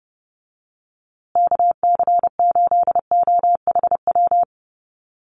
Morse code, heard as a keyed tone, about 30 wpm, 700 Hz. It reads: KC8OHW
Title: USA Call Signs Repeated 3x 30wpm